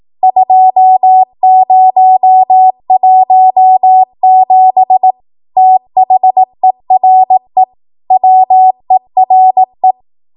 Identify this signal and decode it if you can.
Morse code, heard as a keyed tone, about 18 wpm, 750 Hz. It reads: 2017 THERE WERE
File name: Archive / 18 WPM / 18WPM